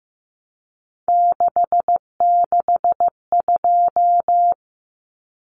Morse code, heard as a keyed tone, about 15 wpm, 700 Hz. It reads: 662
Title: Common Cut Numbers - 3-Digit 15wpm